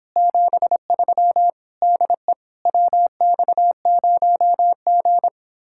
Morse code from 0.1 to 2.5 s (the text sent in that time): 73 DE